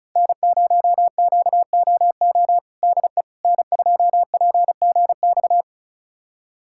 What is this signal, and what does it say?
Morse code, heard as a keyed tone, about 36 wpm, 700 Hz. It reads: N0QOO DE N2PGX
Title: USA Callsign DE Callsign - 5 Characters - 35wpm